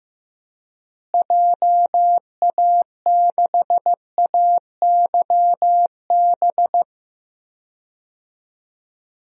Morse code, heard as a keyed tone, about 15 wpm, 700 Hz. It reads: JA6AYB
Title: DXCC-100 Call Signs - 6 Characters - Repeated 2x - 15wpm